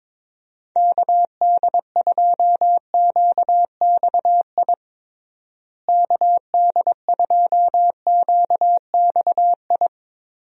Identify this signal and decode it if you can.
Morse code, heard as a keyed tone, about 22 wpm, 700 Hz. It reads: KD2QXI KD2QXI
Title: USA Call Signs - 6 Characters - Repeated 3x - 22wpm